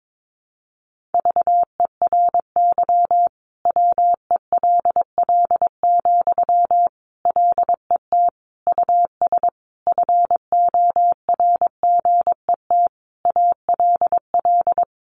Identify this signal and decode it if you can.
Morse code, heard as a keyed tone, about 22 wpm, 700 Hz. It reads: VERY WELL, LET US FORGET ALL
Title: Sentences from Top 1200 Words 22wpm